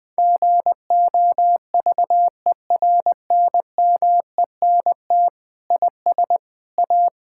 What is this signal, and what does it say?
Morse code, heard as a keyed tone, about 21 wpm, 700 Hz. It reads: GOVERNMENT IS A
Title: Sentences from Top 1000 Words - Review Part 6 - 20wpm